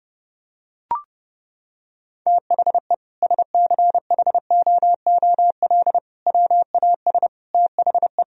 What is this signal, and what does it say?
Morse code, heard as a keyed tone, about 30 wpm, 700 Hz. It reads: THE SCHOOL WAS THE
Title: Sentences from Top 1000 Words - Review Part 2 - 30wpm